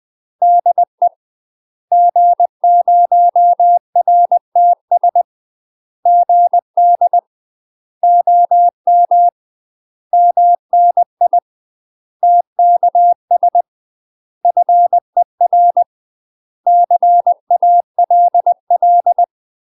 Morse code, heard as a keyed tone, about 20 wpm, 700 Hz. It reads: DE G0RTS GD OM MNI TKS FER CALL
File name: UK MASTER QSOs TRAINING / 20WPM / UK MASTER QSOs 0064